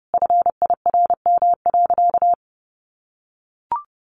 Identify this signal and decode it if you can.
Morse code, heard as a keyed tone, about 30 wpm, 700 Hz. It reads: FIRM.
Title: Sentences from Top 1400 Words 30wpm